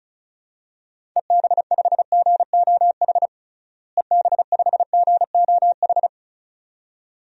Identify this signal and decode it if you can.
Morse code, heard as a keyed tone, about 36 wpm, 700 Hz. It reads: EB5GOH EB5GOH
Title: DXCC-100 Call Signs - 6 Characters - Repeated 2x - 35wpm